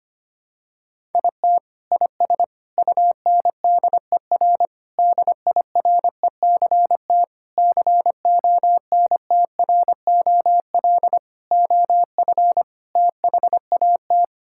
Morse code, heard as a keyed tone, about 25 wpm, 700 Hz. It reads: IT IS UNDER DIRECT CONTROL OF THAT
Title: Sentences from Top 1000 Words - Review Part 5 - 25wpm